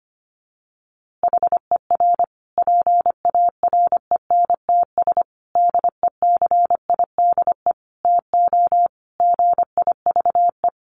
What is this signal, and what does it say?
Morse code, heard as a keyed tone, about 25 wpm, 700 Hz. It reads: HER PARENTS DECIDE TO GIVE
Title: Easy Phrases 25wpm